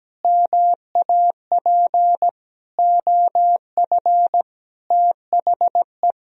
Morse code, heard as a keyed tone, about 17 wpm, 700 Hz. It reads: MAP OF THE
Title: Sentences from Top 1000 Words - Review Part 4 - 17wpm